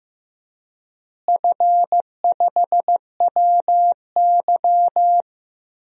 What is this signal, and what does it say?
Morse code, heard as a keyed tone, about 16 wpm, 700 Hz. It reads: F5WY
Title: DXCC-100 Call Signs - 4 Characters - Repeated 3x - 15wpm